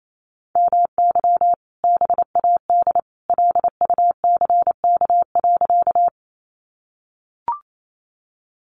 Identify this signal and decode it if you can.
Morse code, heard as a keyed tone, about 28 wpm, 700 Hz. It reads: MY BAD LUCK.
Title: Sentences from Top 1500 Words 28wpm